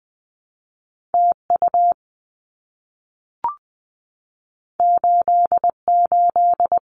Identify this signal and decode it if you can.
Morse code, heard as a keyed tone, about 20 wpm, 700 Hz. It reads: TU 88
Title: Abbreviations Repeated 3x - 20wpm